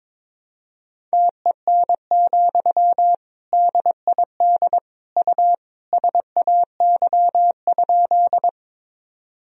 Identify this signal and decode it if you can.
Morse code, heard as a keyed tone, about 21 wpm, 700 Hz. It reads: TEN, DID U SAY?
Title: Sentences from Top 400 Words and Abbreviations 22wpm